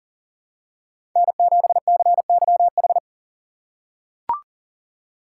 Morse code, heard as a keyed tone, about 40 wpm, 700 Hz. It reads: N7CYH
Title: USA Call Signs - 5 Characters - 6x Character Spacing - 40wpm